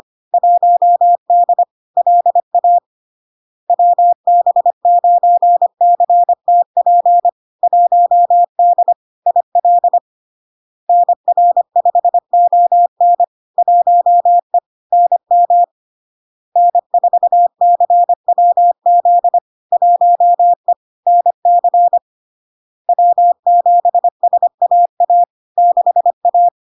Morse code, heard as a keyed tone, about 25 wpm, 700 Hz. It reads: 1D LA WB9CTP 1D IL NR5ON 1E NM N4CWZ 1E NC W7SAA 6A
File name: Log-in-CW-25wpm-700-Hz